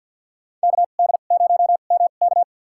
Morse code, heard as a keyed tone, about 53 wpm, spaced as for 37 wpm, 700 Hz. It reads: KD0MK
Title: USA Call Signs - 5 Characters - 2x Character Spacing - 50wpm